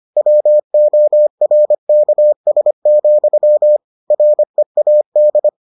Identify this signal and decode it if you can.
Morse code, heard as a keyed tone, about 25 wpm, 600 Hz. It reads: WORKS, READ